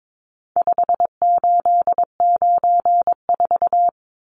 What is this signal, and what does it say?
Morse code, heard as a keyed tone, about 22 wpm, 700 Hz. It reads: 5894
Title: Numbers - 4-Digit 22wpm